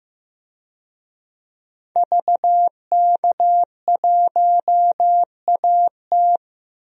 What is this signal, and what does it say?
Morse code, heard as a keyed tone, about 15 wpm, 700 Hz. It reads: VK1AT
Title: DXCC-100 Call Signs - 5 Characters - 15wpm